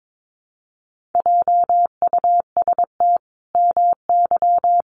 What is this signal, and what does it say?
Morse code, heard as a keyed tone, about 22 wpm, 700 Hz. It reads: JUST MY
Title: Sentences from Top 1200 Words 22wpm